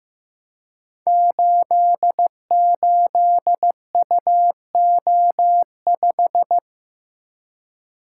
Morse code, heard as a keyed tone, about 15 wpm, 700 Hz. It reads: 88UO5
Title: Code Groups - 5 Letter:Numbers - 15wpm